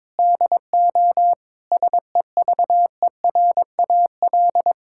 Morse code, heard as a keyed tone, about 22 wpm, 700 Hz. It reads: DO SEVERAL
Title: Sentences from Top 1000 Words - Review Part 6 - 22wpm